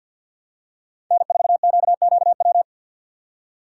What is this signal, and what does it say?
Morse code, heard as a keyed tone, about 46 wpm, 700 Hz. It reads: N4QQW
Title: USA Call Signs - 5 Characters - 3x Character Spacing - 50wpm